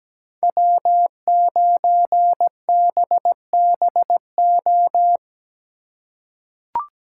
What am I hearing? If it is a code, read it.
Morse code, heard as a keyed tone, about 17 wpm, 700 Hz. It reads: W9BBO
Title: USA Call Signs - 5 Characters - 4x Character Spacing - 17wpm